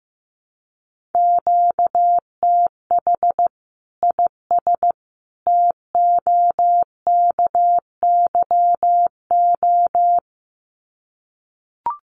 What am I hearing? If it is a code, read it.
Morse code, heard as a keyed tone, about 15 wpm, 700 Hz. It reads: QTH IS TOKYO
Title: Common QSO Phrases 15wpm